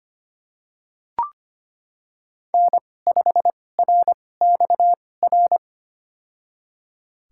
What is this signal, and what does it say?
Morse code, heard as a keyed tone, about 25 wpm, spaced as for 18 wpm, 700 Hz. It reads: N5RXR
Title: USA Call Signs - 5 Characters - 2x Character Spacing - 25wpm